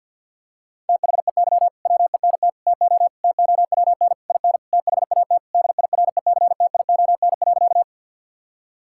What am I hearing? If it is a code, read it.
Morse code, heard as a keyed tone, about 52 wpm, 700 Hz. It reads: THEY WENT TO TOWN IN THAT DIRECTION.
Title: Sentences from Top 1000 Words - Review Part 5 - 50wpm